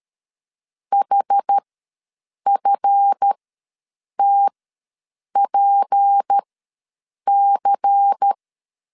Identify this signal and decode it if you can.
Morse code, heard as a keyed tone, about 13 wpm, spaced as for 7 wpm, 800 Hz. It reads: HFTPC
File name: LEZIONI / 7 GF9Z / 7 GF9Z ws 12wpm 3321